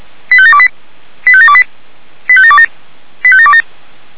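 Five-tone selective call (ZVEI), heard as quick runs of five tones, tones of about 70 ms. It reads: 85728 85728 85727 85727
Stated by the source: ZVEI five-tone call received by SDR.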